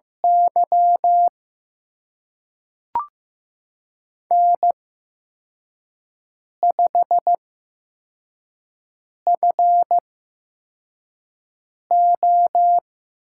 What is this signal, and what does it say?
Morse code, heard as a keyed tone, about 15 wpm, spaced as for 4 wpm, 700 Hz. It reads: Y N5FO
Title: USA Call Signs - 4 Characters - 8x Character Spacing - 15wpm